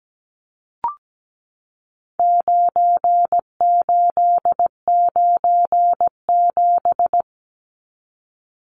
Morse code, heard as a keyed tone, about 17 wpm, 700 Hz. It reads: 9897
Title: Numbers - 4-Digit 17wpm